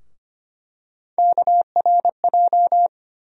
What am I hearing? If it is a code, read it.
Morse code, heard as a keyed tone, about 25 wpm, 700 Hz. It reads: KRJ
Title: USA Call Sign Postfixes - 3 Letters 25wpm